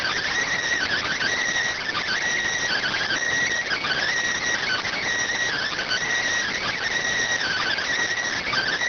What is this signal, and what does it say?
Emergency Alert System (SAME) tones (header bursts), heard as harsh buzzing bursts. It is heard from 0.7 to 8.1 s.